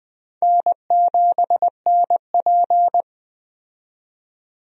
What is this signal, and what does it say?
Morse code, heard as a keyed tone, about 21 wpm, 700 Hz. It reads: N7NP
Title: USA Call Signs - 4 Characters - 20wpm